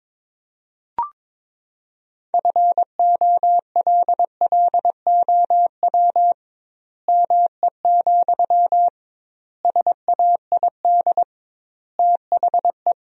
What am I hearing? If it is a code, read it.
Morse code, heard as a keyed tone, about 22 wpm, 700 Hz. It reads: FOLLOW ME, SAID THE
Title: 2x Word-Spacing - Sentences from Top 200 Words - 22wpm